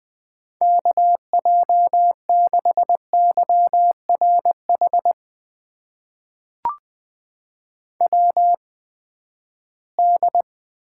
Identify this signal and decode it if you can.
Morse code, heard as a keyed tone, about 20 wpm, spaced as for 14 wpm, 700 Hz. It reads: KJ6YRH WD
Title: USA Call Signs - 6 Characters - 8x Character Spacing - 20wpm